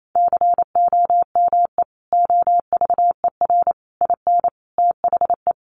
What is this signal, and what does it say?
Morse code, heard as a keyed tone, about 28 wpm, 700 Hz. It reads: COME OVER IN THE
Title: Sentences from Top 300 Words 28wpm